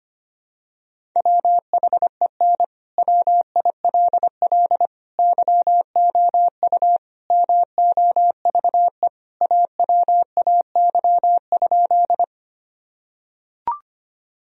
Morse code, heard as a keyed tone, about 25 wpm, 700 Hz. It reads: WHEN WILL YOU MOVE AWAY?
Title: Sentences from Top 1000 Words - Review Part 1 - 25wpm